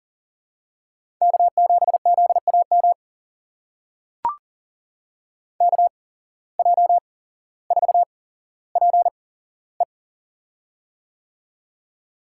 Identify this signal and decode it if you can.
Morse code, heard as a keyed tone, about 39 wpm, spaced as for 17 wpm, 700 Hz. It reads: K7ZAM KJ4PE
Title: USA Call Signs - 5 Characters - 8x Character Spacing - 40wpm